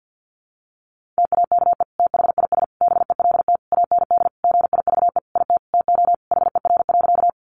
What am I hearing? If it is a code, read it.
Morse code, heard as a keyed tone, about 50 wpm, 700 Hz. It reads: TAKE THIS BELT AND GIVE IT TO HER.
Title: Sentences from Top 1400 Words 50wpm